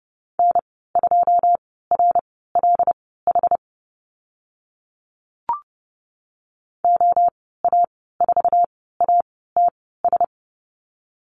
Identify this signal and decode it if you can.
Morse code, heard as a keyed tone, about 30 wpm, spaced as for 17 wpm, 700 Hz. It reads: N2RLH OA4ATS
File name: Lesson 15g - Number 5 - Call Signs 30wpm